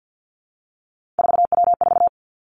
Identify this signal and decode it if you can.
Morse code, heard as a keyed tone, about 50 wpm, 700 Hz. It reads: 4W4
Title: Three Challenging Letter-Number Combinations 50wpm